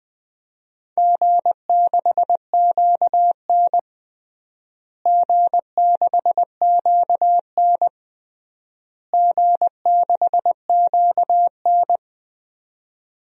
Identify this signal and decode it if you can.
Morse code, heard as a keyed tone, about 20 wpm, 700 Hz. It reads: G6QN G6QN G6QN
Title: DXCC-100 Call Signs - 4 Characters - Repeated 3x - 20wpm